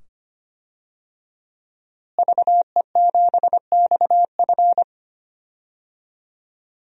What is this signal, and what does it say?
Morse code, heard as a keyed tone, about 25 wpm, 700 Hz. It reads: VE7XF
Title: DXCC-100 Call Signs - 5 Characters - Repeated 2x - 25wpm